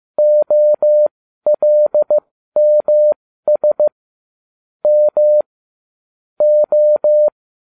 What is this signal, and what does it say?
Morse code, heard as a keyed tone, about 15 wpm, spaced as for 12 wpm, 600 Hz. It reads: OLMS M O